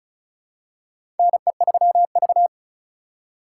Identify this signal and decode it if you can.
Morse code, heard as a keyed tone, about 36 wpm, 700 Hz. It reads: NE3V
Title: USA Call Signs - 3x Character Spacing - 35wpm